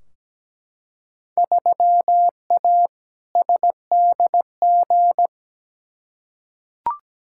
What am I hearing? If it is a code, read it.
Morse code, heard as a keyed tone, about 17 wpm, 700 Hz. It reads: 3A SDG
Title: Field Day Exchanges - 17wpm